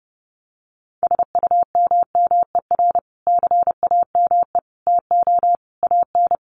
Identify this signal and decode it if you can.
Morse code, heard as a keyed tone, about 30 wpm, 700 Hz. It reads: SUMMER CAME TO AN
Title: Sentences from Top 1100 Words 30wpm